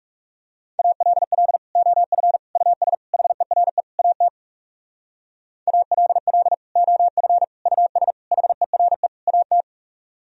Morse code, heard as a keyed tone, about 40 wpm, 700 Hz. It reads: ALL OF US HERE AT ALL OF US HERE AT